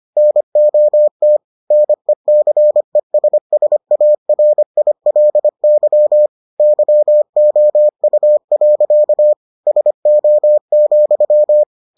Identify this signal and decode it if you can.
Morse code, heard as a keyed tone, about 25 wpm, 600 Hz. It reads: NOT NECESSARILY YOU. SO,